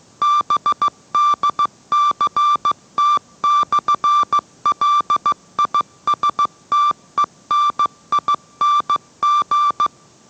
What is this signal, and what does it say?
Morse code, heard as a keyed tone, about 15 wpm, 1170 Hz. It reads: BDCT/LISTENING